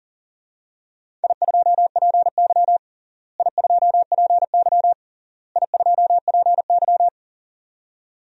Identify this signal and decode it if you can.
Morse code, heard as a keyed tone, about 40 wpm, 700 Hz. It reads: I2PY I2PY I2PY
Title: DXCC-100 Call Signs - 4 Characters - Repeated 3x - 40wpm